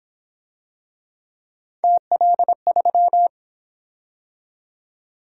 Morse code, heard as a keyed tone, about 28 wpm, 700 Hz. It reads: TL3